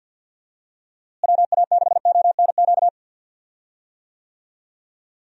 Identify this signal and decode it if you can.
Morse code, heard as a keyed tone, about 53 wpm, 700 Hz. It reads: WA6ONQ
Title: USA Call Signs - 6 Characters - Repeated 2x - 50wpm